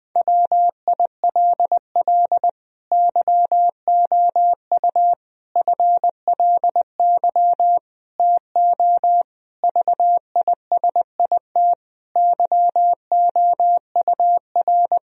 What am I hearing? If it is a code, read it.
Morse code, heard as a keyed tone, about 21 wpm, 700 Hz. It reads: WILL YOU FLY TO VISIT YOUR
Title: Sentences from Top 1000 Words - Review Part 2 - 20wpm